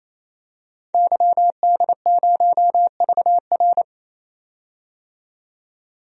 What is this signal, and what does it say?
Morse code, heard as a keyed tone, about 28 wpm, 700 Hz. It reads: YD0VR
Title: DXCC-100 Call Signs - 5 Characters - 28wpm